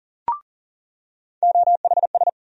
Morse code, heard as a keyed tone, about 40 wpm, 700 Hz. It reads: OHS
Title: USA Call Sign Postfixes - 3 Letters 40wpm